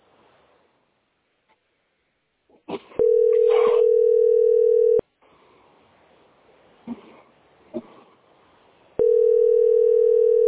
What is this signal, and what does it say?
Ringback tone (North America), 440 + 480 Hz.